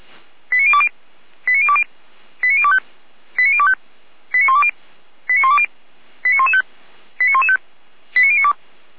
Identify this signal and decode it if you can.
Five-tone selective call (ZVEI), heard as quick runs of five tones, tones of about 70 ms. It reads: 89020 89020 89025 89025 89120 89120 89115 89115 89092
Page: ZVEI